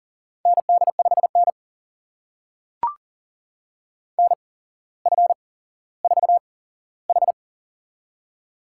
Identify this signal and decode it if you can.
Morse code, heard as a keyed tone, about 40 wpm, spaced as for 17 wpm, 700 Hz. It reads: ND5N NF4H
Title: USA Call Signs - 4 Characters - 8x Character Spacing - 40wpm